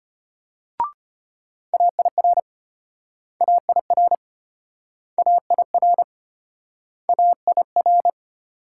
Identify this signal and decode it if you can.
Morse code, heard as a keyed tone, about 33 wpm, 700 Hz. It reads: AIR AIR AIR AIR